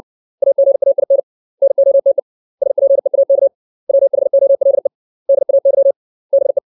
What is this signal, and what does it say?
Morse code, heard as a keyed tone, about 60 wpm, 550 Hz. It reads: AGAIN NOTE SPEAK WHOLE DAY BE
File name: 632 words / 632 Words 60WPM